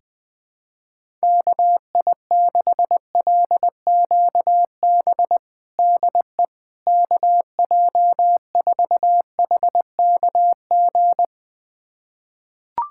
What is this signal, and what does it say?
Morse code, heard as a keyed tone, about 20 wpm, 700 Hz. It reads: KI6LQB DE KJ4HKG
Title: USA Callsign DE Callsign - 6 Characters - 20wpm